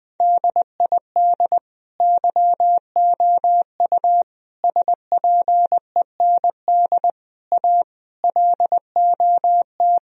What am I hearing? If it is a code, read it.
Morse code, heard as a keyed tone, about 20 wpm, 700 Hz. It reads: DID YOU SPEND A LOT
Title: Sentences from Top 1000 Words - Review Part 4 - 20wpm